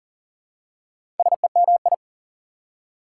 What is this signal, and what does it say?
Morse code, heard as a keyed tone, about 40 wpm, 700 Hz. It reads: SEMI
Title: Prefixes and Suffixes 40wpm